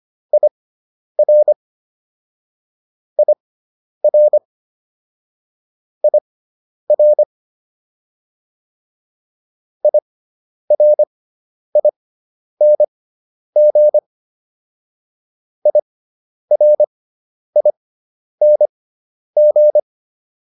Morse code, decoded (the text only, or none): IR IR IR IRING IRING